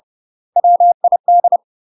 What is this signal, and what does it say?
Morse code, heard as a keyed tone, about 30 wpm, 700 Hz. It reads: WID
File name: UK MASTER QSOs TRAINING / 30WPM / UK MASTER QSOs 0000